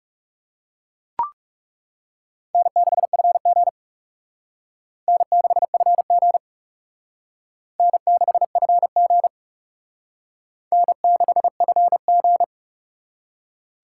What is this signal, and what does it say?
Morse code, heard as a keyed tone, about 35 wpm, 700 Hz. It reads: N6FG N6FG N6FG N6FG
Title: Speed Racing - USA 1x2 Call Signs 30wpm